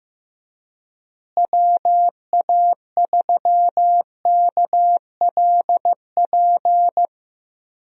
Morse code, heard as a keyed tone, about 15 wpm, 700 Hz. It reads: WA3KLP
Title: USA Call Signs - 6 Characters - Repeated 2x - 15wpm